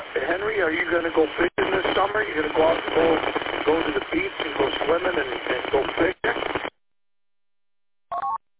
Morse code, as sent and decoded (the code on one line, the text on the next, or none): none